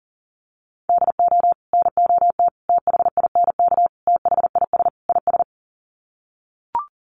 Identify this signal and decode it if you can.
Morse code, heard as a keyed tone, about 40 wpm, 700 Hz. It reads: DO NOT THINK THIS IS
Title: Sets of 5 Words Encore2 40wpm